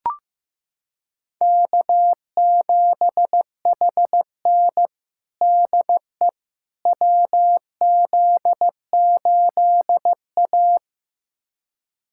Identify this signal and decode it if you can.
Morse code, heard as a keyed tone, about 15 wpm, 700 Hz. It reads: K7HN DE WZ8A
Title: USA Callsign DE Callsign - 4 Characters - 15wpm